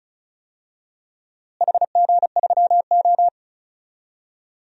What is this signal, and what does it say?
Morse code, heard as a keyed tone, about 36 wpm, 700 Hz. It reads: HG3O
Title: DXCC-100 Call Signs - 4 Characters - Repeated 3x - 35wpm